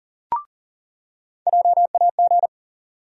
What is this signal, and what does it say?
Morse code, heard as a keyed tone, about 40 wpm, 700 Hz. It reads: JAG
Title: USA Call Sign Postfixes - 3 Letters 40wpm